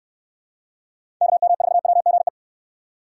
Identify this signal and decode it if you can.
Morse code, heard as a keyed tone, about 69 wpm, 700 Hz. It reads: KM4WGE